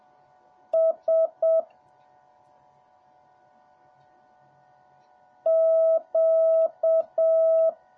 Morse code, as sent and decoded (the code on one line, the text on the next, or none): ... / --.-
S Q